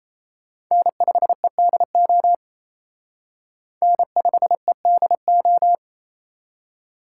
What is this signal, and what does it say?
Morse code, heard as a keyed tone, about 34 wpm, 700 Hz. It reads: N5EDO N5EDO
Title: Speed Racing - USA Call Signs 28wpm